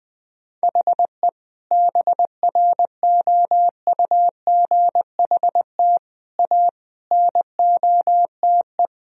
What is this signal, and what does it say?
Morse code, heard as a keyed tone, about 20 wpm, 700 Hz. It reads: HE BROUGHT A NOTE
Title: Sentences from Top 1000 Words - Review Part 2 - 20wpm